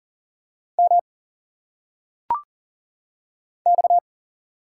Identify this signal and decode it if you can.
Morse code, heard as a keyed tone, about 38 wpm, 700 Hz. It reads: M X
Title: Single Letters - Phonetic Pronunciation 40wpm